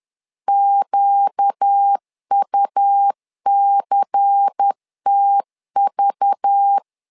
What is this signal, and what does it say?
Morse code, heard as a keyed tone, about 10 wpm, 800 Hz. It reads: QUCTV
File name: LEZIONI / 5 LBVC / 5 LBVC 10wpm 2349